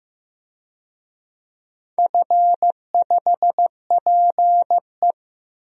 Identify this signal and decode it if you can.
Morse code, heard as a keyed tone, about 15 wpm, 700 Hz. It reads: F5PE
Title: DXCC-100 Call Signs - 4 Characters - Repeated 2x - 15wpm